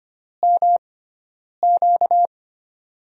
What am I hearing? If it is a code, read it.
Morse code, heard as a keyed tone, about 25 wpm, spaced as for 9 wpm, 700 Hz. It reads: MQ
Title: USA Call Signs - 4 Characters - 6x Character Spacing - 25wpm